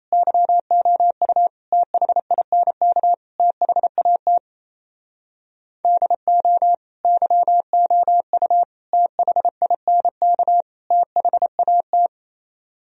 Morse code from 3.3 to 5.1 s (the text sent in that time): THAT